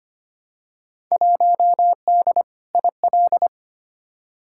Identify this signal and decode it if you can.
Morse code, heard as a keyed tone, about 25 wpm, 700 Hz. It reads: 1D IL
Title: Field Day Exchanges - 25wpm